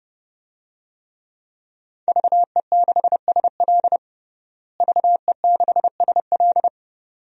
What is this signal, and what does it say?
Morse code, heard as a keyed tone, about 32 wpm, 700 Hz. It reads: VE6SL VE6SL
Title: DXCC-100 Call Signs - 5 Characters - Repeated 2x - 30wpm